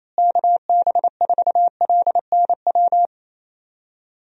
Morse code, heard as a keyed tone, about 28 wpm, 700 Hz. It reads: KB4LNW
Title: USA Call Signs - 6 Characters - Repeated 2x - 28wpm